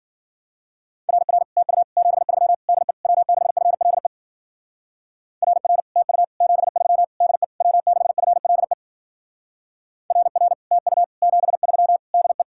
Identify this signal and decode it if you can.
Morse code, heard as a keyed tone, about 52 wpm, 700 Hz. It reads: RR TU 73 DE W6FLE RR TU 73 DE W6FLE RR TU 73 DE